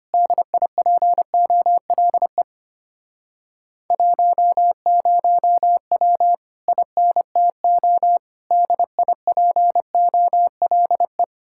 Morse code, heard as a keyed tone, about 33 wpm, 700 Hz. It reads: DIPOLE 10W INTO DIPOLE